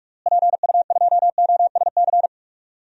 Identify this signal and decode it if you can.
Morse code, heard as a keyed tone, about 41 wpm, 700 Hz. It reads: PU2OSC
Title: DXCC-100 Call Signs - 6 Characters - Repeated 2x - 45wpm